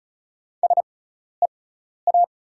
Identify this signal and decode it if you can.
Morse code, heard as a keyed tone, about 34 wpm, spaced as for 12 wpm, 690 Hz. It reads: SEA
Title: 6x Character Spacing - Top 300 Words - 35wpm